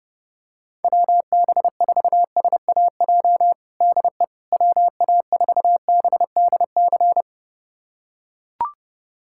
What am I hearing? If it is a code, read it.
Morse code, heard as a keyed tone, about 32 wpm, 700 Hz. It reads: WB4SAJ DE WA4BDC
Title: USA Callsign DE Callsign - 6 Characters - 30wpm